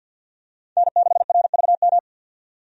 Morse code, heard as a keyed tone, about 51 wpm, 700 Hz. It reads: N6RVM
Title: USA Call Signs - 5 Characters - 50wpm